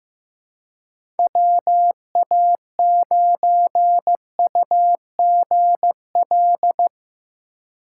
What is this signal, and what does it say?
Morse code, heard as a keyed tone, about 15 wpm, 700 Hz. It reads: WA9UGL
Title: USA Call Signs - 6 Characters - 3x Character Spacing - 15wpm